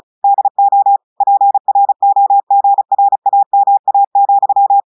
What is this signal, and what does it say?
Morse code, heard as a keyed tone, about 36 wpm, 800 Hz. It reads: DO PROGRAMA,
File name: audios / portugais / portugais texte en clair 2 vitesse 35 mots minutes